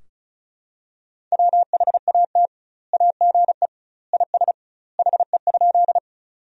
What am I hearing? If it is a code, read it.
Morse code, heard as a keyed tone, about 36 wpm, 700 Hz. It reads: WHAT AGE IS HE?
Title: 2x Word-Spacing - Sentences from Top 500 Words - 35wpm